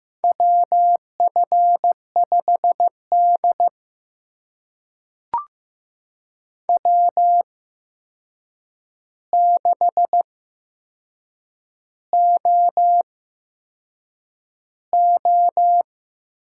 Morse code, heard as a keyed tone, about 15 wpm, spaced as for 6 wpm, 700 Hz. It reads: WF5D W6OO
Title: USA Call Signs - 4 Characters - 8x Character Spacing - 15wpm